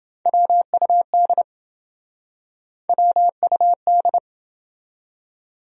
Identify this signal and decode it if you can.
Morse code, heard as a keyed tone, about 27 wpm, 700 Hz. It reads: WUD WUD